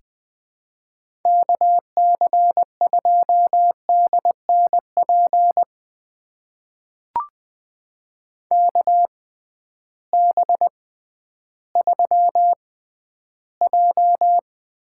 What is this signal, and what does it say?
Morse code, heard as a keyed tone, about 21 wpm, spaced as for 12 wpm, 700 Hz. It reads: KC2DNP KB3J
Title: USA Call Signs - 6 Characters - 6x Character Spacing - 20wpm